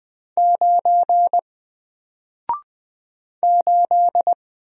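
Morse code, heard as a keyed tone, about 20 wpm, 700 Hz. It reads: 9 8
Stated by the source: Single letter and number Morse drill at 20 wpm, from N4LCD.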